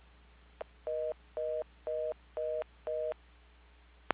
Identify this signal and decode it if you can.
Reorder tone (North America), 480 + 620 Hz.
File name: recordings / PortlandUnion Millennium